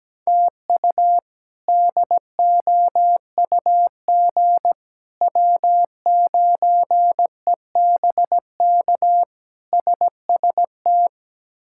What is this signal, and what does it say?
Morse code, heard as a keyed tone, about 17 wpm, 700 Hz. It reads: TU DOUG W9EBK SST
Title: K1USN SST QSO Practice - 17wpm